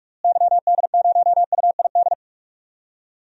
Morse code, heard as a keyed tone, about 46 wpm, 700 Hz. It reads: YD0UID
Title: DXCC-100 Call Signs - 6 Characters - Repeated 3x - 45wpm